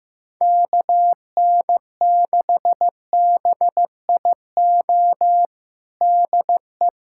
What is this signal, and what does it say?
Morse code, heard as a keyed tone, about 15 wpm, 700 Hz. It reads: KN6BIO DE
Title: USA Callsign DE Callsign - 6 Characters - 15wpm